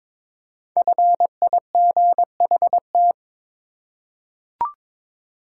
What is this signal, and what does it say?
Morse code, heard as a keyed tone, about 22 wpm, 700 Hz. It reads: FIGHT
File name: 5 Letter Words - First 3 Letters Given - 22wpm